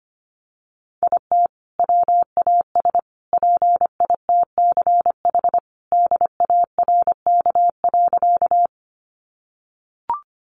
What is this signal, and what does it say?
Morse code, heard as a keyed tone, about 25 wpm, 700 Hz. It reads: IT WAS PITCH DARK.
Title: Sentences from Top 1300 Words 25wpm